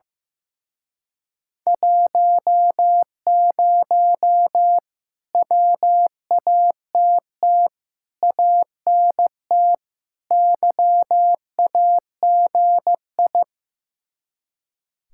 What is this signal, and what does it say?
Morse code, heard as a keyed tone, about 16 wpm, 700 Hz. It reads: 10 WATT ANT YAGI
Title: Speed-Racing Common QSO Phrases Encore - 15wpm